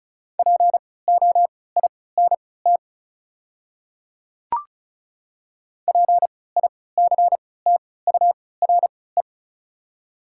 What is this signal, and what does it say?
Morse code, heard as a keyed tone, about 36 wpm, spaced as for 20 wpm, 700 Hz. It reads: POINT PICTURE
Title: 3x Character Spacing - Top 300 Words - 35wpm